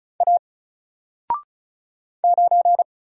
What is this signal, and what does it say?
Morse code, heard as a keyed tone, about 36 wpm, 700 Hz. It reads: A 9
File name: Single-Letter-Number-35wpm